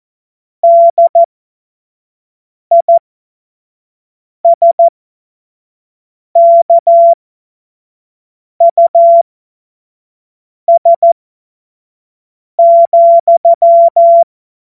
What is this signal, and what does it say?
Morse code, heard as a keyed tone, about 14 wpm, spaced as for 5 wpm, 680 Hz. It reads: DISKUS,
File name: Morsekurs / Witze / Witz 15. August 2023 25 F70